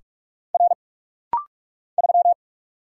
Morse code, heard as a keyed tone, about 46 wpm, 700 Hz. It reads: R 3
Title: Single Letter-Number - Rapid-Fire 45wpm